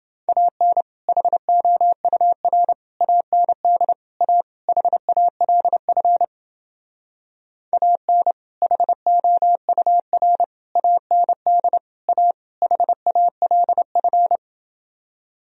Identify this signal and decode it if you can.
Morse code, heard as a keyed tone, about 29 wpm, 700 Hz. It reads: AN HOUR AND A HALF AN HOUR AND A HALF